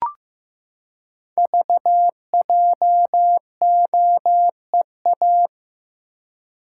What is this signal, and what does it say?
Morse code, heard as a keyed tone, about 15 wpm, 700 Hz. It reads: VJOEA
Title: Code Groups - 5 Letters - 15wpm